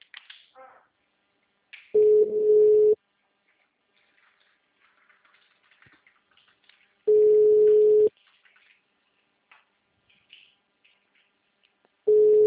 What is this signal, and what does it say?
Ringback tone (Europe, 425 Hz).